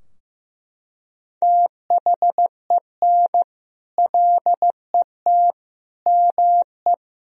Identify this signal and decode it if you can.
Morse code, heard as a keyed tone, about 16 wpm, 700 Hz. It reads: THEN LET ME
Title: Sentences from Top 1000 Words - Review Part 2 - 15wpm